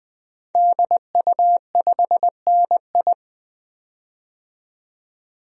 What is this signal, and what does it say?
Morse code, heard as a keyed tone, about 20 wpm, 700 Hz. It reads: DU5NI
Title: DXCC-100 Call Signs - 5 Characters - 20wpm